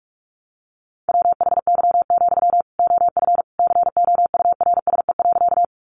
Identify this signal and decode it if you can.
Morse code, heard as a keyed tone, about 44 wpm, 700 Hz. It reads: WHY, OF COURSE.
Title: Sentences from Top 500 Words 45wpm